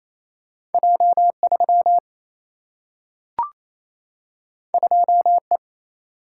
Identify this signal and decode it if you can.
Morse code, heard as a keyed tone, about 28 wpm, 700 Hz. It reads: J3 2E
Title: Single-Letter Single-Number 28wpm